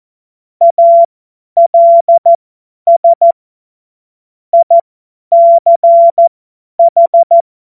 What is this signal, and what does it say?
Morse code, heard as a keyed tone, about 14 wpm, spaced as for 10 wpm, 680 Hz. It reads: ALS ICH
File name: Morsekurs / Witze / Witz 30. August 2022 50 F70